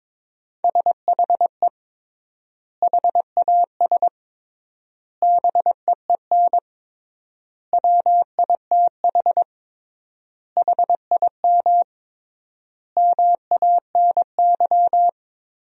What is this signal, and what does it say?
Morse code, heard as a keyed tone, about 22 wpm, 700 Hz. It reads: SHE HAS BEEN WITH HIM MANY
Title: 3x Word-Spacing - Sentences from Top 200 Words - 22wpm